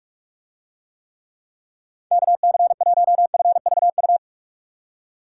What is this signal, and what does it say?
Morse code, heard as a keyed tone, about 46 wpm, 700 Hz. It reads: KC1FVU